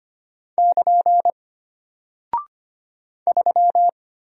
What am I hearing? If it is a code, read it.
Morse code, heard as a keyed tone, about 25 wpm, 700 Hz. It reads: <KN> 3
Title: Single Character (Letter:Number:Punctuation:Prosign) Alt 25wpm